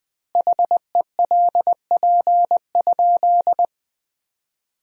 Morse code, heard as a keyed tone, about 20 wpm, 700 Hz. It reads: HELP?
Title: 3x Word-Spacing - Sentences from Top 200 Words - 20wpm